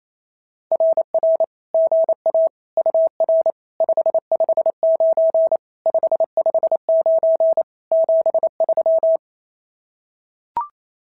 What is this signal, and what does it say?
Morse code, heard as a keyed tone, about 28 wpm, 590 Hz. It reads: RR GA UR 559 559 73
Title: SOTA QSO Practice 28wpm